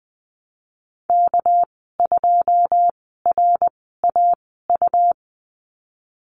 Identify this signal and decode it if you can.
Morse code, heard as a keyed tone, about 20 wpm, spaced as for 15 wpm, 700 Hz. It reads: K2RAU
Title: 213 Lesson 34g - Number 8 - Call Signs 20wpm